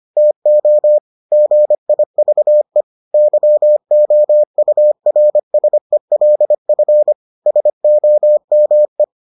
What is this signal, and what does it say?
Morse code, heard as a keyed tone, about 25 wpm, 600 Hz. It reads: TO GIVE YOURSELF SOME